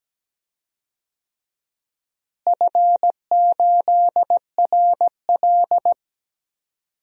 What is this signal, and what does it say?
Morse code, heard as a keyed tone, about 17 wpm, 700 Hz. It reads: F8RL
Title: DXCC-100 Call Signs - 4 Characters - Repeated 2x - 17wpm